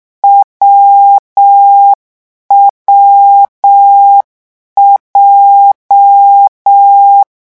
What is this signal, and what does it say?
Morse code, heard as a keyed tone, about 6 wpm, 800 Hz. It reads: WWJ